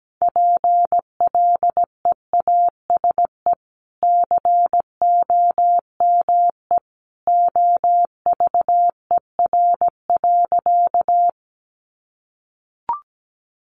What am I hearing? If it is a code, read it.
Morse code, heard as a keyed tone, about 17 wpm, 700 Hz. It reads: PLEASE COME OVER.
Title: Sentences from Top 1500 Words 17wpm